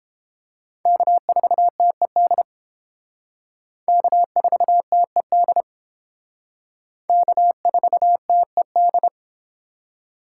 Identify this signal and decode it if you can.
Morse code, heard as a keyed tone, about 31 wpm, 700 Hz. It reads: K4TED K4TED K4TED